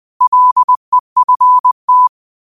Morse code, heard as a keyed tone, about 20 wpm, 1000 Hz. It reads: LEFT
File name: qotdfiles / 20wpm